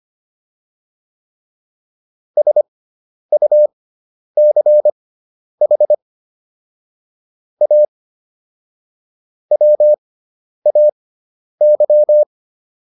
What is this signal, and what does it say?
Morse code, heard as a keyed tone, about 25 wpm, spaced as for 10 wpm, 600 Hz. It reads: SUCH A WAY